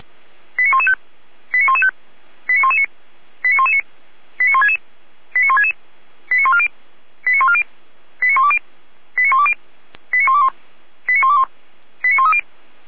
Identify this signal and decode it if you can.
Five-tone selective call (ZVEI), heard as quick runs of five tones, tones of about 70 ms. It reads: 89115 89115 89119 89119 89166 89166 89140 89140 89120 89120 89121 89121 89130